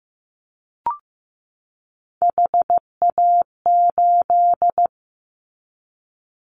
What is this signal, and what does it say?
Morse code, heard as a keyed tone, about 15 wpm, 700 Hz. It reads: HA8
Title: Callsign Prefixes for Top 25 DXCC - 15wpm